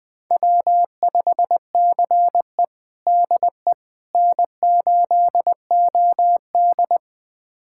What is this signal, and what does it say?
Morse code, heard as a keyed tone, about 20 wpm, 700 Hz. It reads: W5CE DE N8OD
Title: USA Callsign DE Callsign - 4 Characters - 20wpm